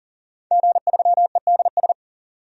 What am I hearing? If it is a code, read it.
Morse code, heard as a keyed tone, about 38 wpm, 700 Hz. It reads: G3EDS
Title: DXCC-100 Call Signs - 5 Characters - 40wpm